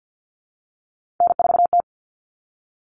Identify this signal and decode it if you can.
Morse code, heard as a keyed tone, about 50 wpm, 700 Hz. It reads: N4T
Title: All Cut Numbers - 3-Digit 50wpm